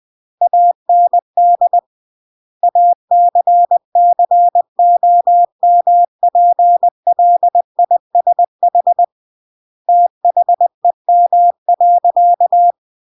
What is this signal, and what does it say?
Morse code, heard as a keyed tone, about 20 wpm, 700 Hz. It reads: AND ACCOMPLISH THEM.
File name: CW / jodorowsky / 82 commandments 20-20